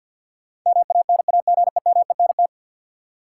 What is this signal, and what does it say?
Morse code, heard as a keyed tone, about 52 wpm, 700 Hz. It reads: MANAGEMENT
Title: Sentences from Top 1000 Words - Review Part 4 - New Word Drill - 50wpm